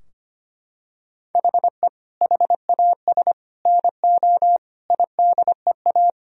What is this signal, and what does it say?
Morse code, heard as a keyed tone, about 25 wpm, 700 Hz. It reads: HE HAS NO IDEA
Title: Sentences from Top 1000 Words - Review Part 3 - 25wpm